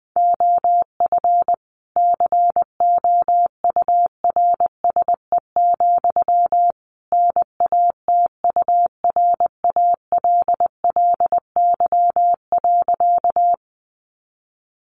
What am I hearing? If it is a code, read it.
Morse code, heard as a keyed tone, about 20 wpm, 700 Hz. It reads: OF COURSE, NATURALLY.
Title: Sentences from Top 1300 Words 20wpm